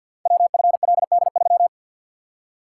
Morse code, heard as a keyed tone, about 50 wpm, 700 Hz. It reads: WFLD3
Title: Code Groups - 5 Letter:Numbers - 50wpm